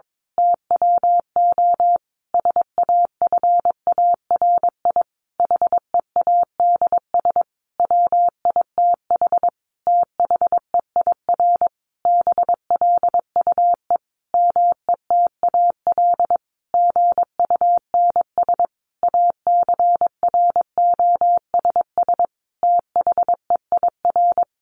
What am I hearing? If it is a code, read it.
Morse code, heard as a keyed tone, about 22 wpm, 700 Hz. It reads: TWO SAFARI HEADS WITH THEIR BLUE METAL GUNS ACROSS THEIR
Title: A Sound Of Thunder 22wpm